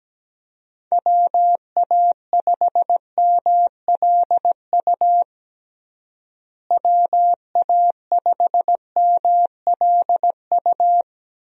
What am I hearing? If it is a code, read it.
Morse code, heard as a keyed tone, about 17 wpm, 700 Hz. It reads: WA5MLU WA5MLU
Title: USA Call Signs - 6 Characters - Repeated 3x - 17wpm